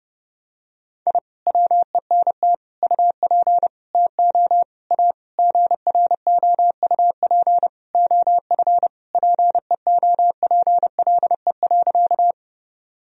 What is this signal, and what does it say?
Morse code, heard as a keyed tone, about 30 wpm, 700 Hz. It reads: I WENT UP TO A GROUP OF PEOPLE.
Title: Sentences from Top 1000 Words - Review Part 1 - 30wpm